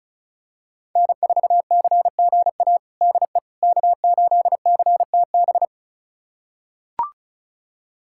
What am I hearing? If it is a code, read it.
Morse code, heard as a keyed tone, about 36 wpm, 700 Hz. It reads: N4CGA DE K8CTB
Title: USA Callsign DE Callsign - 5 Characters - 35wpm